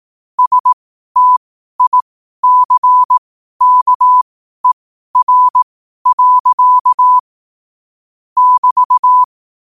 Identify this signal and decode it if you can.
Morse code, heard as a keyed tone, about 18 wpm, spaced as for 12 wpm, 1000 Hz. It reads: STICKER. <BT>
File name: qotdfiles / 12wpm